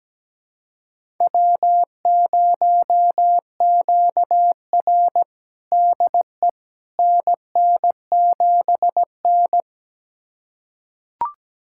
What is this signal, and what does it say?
Morse code, heard as a keyed tone, about 17 wpm, 700 Hz. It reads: W0QR DE NN7N
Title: USA Callsign DE Callsign - 4 Characters - 17wpm